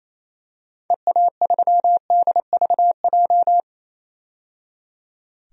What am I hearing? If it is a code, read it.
Morse code, heard as a keyed tone, about 28 wpm, 700 Hz. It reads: EA3DVJ
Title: DXCC-100 Call Signs - 6 Characters - 28wpm